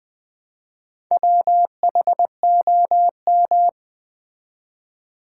Morse code, heard as a keyed tone, about 20 wpm, 700 Hz. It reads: WHOM
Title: Build A Word - 1 Letter At A Time - 4-Letter Words - 20wpm